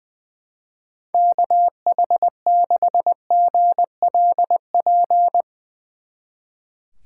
Morse code, heard as a keyed tone, about 20 wpm, 700 Hz. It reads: KH6GLP
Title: Speed Racing - USA 2x3 Call Signs 20wpm